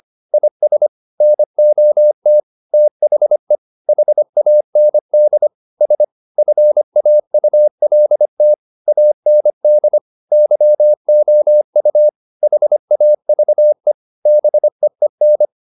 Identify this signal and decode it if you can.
Morse code, heard as a keyed tone, about 25 wpm, 600 Hz. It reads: IS NOT THE HAND S FAULT AND YOU HAVE BEEN
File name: the old man and the sea part 1 25wpm